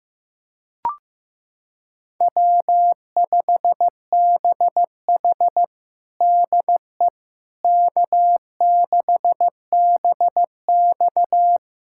Morse code, heard as a keyed tone, about 15 wpm, 700 Hz. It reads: W5BH DE K6BX
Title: USA Callsign DE Callsign - 4 Characters - 15wpm